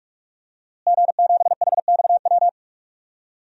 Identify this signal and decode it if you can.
Morse code, heard as a keyed tone, about 44 wpm, 700 Hz. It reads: G7HXW
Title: DXCC-100 Call Signs - 5 Characters - Repeated 2x - 45wpm